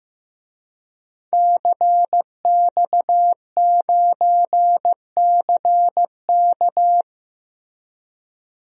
Morse code, heard as a keyed tone, about 15 wpm, 700 Hz. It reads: CX9CK
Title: DXCC-100 Call Signs - 5 Characters - Repeated 2x - 15wpm